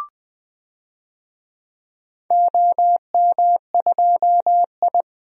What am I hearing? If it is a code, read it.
Morse code, heard as a keyed tone, about 20 wpm, 700 Hz. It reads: OM2I
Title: DXCC-100 Call Signs - 4 Characters - 20wpm